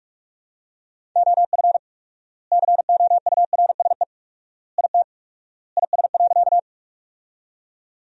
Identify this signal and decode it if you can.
Morse code, heard as a keyed tone, about 47 wpm, 690 Hz. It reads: OF COURSE IT IS.
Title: 4x Word-Spacing - Sentences from Top 500 Words - 45wpm